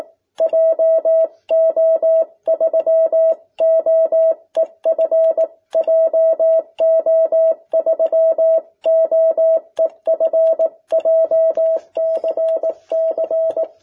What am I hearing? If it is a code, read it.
Morse code, heard as a keyed tone, about 18 wpm, 650 Hz. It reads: JO3OEF JO3OEF JCC
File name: jh8cbh / 2021cw